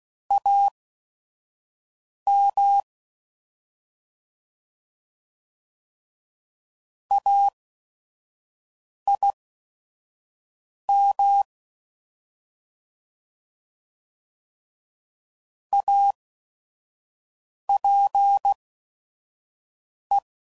Morse code, heard as a keyed tone, about 16 wpm, spaced as for 5 wpm, 800 Hz. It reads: AM AIM APE
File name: Prefixes / By First Letter / As